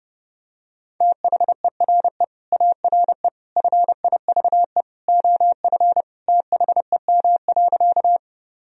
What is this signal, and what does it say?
Morse code, heard as a keyed tone, about 32 wpm, 700 Hz. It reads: THERE ARE FIVE OF THEM.
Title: Sentences from Top 1000 Words - Review Part 2 - 30wpm